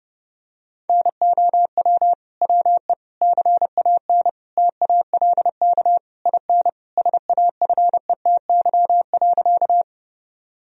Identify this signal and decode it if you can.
Morse code, heard as a keyed tone, about 30 wpm, 700 Hz. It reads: NOW WE CAN TALK IN SAFETY.
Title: Sentences from Top 1000 Words - Review Part 4 - 30wpm